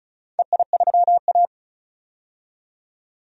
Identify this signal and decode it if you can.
Morse code, heard as a keyed tone, about 36 wpm, 700 Hz. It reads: EI3A
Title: DXCC-100 Call Signs - 4 Characters - 35wpm